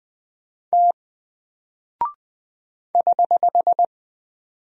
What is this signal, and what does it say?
Morse code, heard as a keyed tone, about 20 wpm, 700 Hz. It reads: T <HH>
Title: Single Character (Letter:Number:Punctuation:Prosign) Alt 20wpm